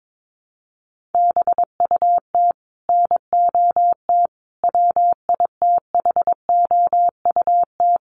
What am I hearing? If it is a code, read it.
Morse code, heard as a keyed tone, about 22 wpm, 700 Hz. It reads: BUT NOT WITHOUT
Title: Sentences from Top 1500 Words 22wpm